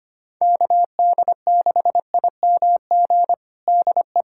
Morse code, heard as a keyed tone, about 25 wpm, 700 Hz. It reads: KD6IMG DE
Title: USA Callsign DE Callsign 25wpm